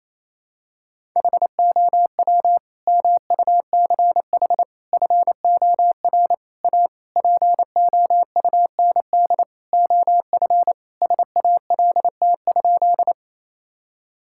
Morse code, heard as a keyed tone, about 28 wpm, 700 Hz. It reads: HOW MUCH FOR A POUND OF SALT?
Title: Sentences from Top 1000 Words - Review Part 5 - 28wpm